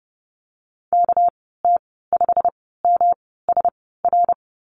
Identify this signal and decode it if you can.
Morse code, heard as a keyed tone, about 30 wpm, spaced as for 17 wpm, 700 Hz. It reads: KT5MSR
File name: Lesson 37g - Letter x - Call Signs 30wpm